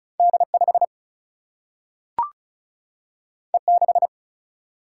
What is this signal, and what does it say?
Morse code, heard as a keyed tone, about 36 wpm, 700 Hz. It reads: D 5 E 6
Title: Single-Letter Single-Number 35wpm